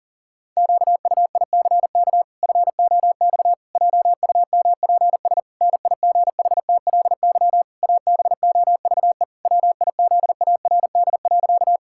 Morse code, heard as a keyed tone, about 37 wpm, 700 Hz. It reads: QUICK FOX JUMPS NIGHTLY ABOVE WIZARD.
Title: Pangram Challenge v2 - 40wpm